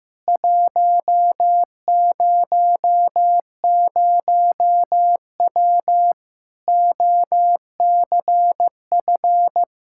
Morse code, heard as a keyed tone, about 15 wpm, 700 Hz. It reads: 100W OCF
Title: Practice QSO - Exchange 2 - Info - 15wpm